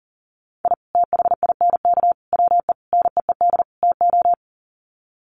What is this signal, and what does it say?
Morse code, heard as a keyed tone, about 37 wpm, 700 Hz. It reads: I THINK WE NEED TO
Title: Sets of 5 Words 40wpm